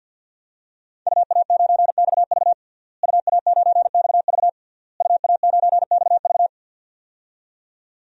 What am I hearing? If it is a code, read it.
Morse code, heard as a keyed tone, about 51 wpm, 700 Hz. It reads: UA9XV UA9XV UA9XV
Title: DXCC-100 Call Signs - 5 Characters - Repeated 3x - 50wpm